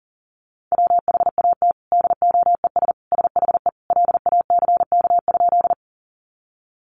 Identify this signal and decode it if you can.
Morse code, heard as a keyed tone, about 40 wpm, 700 Hz. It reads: WHAT DOES SHE LACK?
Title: Sentences from Top 1100 Words 40wpm